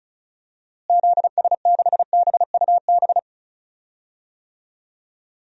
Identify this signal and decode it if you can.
Morse code, heard as a keyed tone, about 36 wpm, 700 Hz. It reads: ZS6BUB
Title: DXCC-100 Call Signs - 6 Characters - Repeated 2x - 35wpm